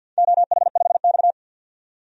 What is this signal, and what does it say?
Morse code, heard as a keyed tone, about 52 wpm, 700 Hz. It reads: OH5X
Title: DXCC-100 Call Signs - 4 Characters - 50wpm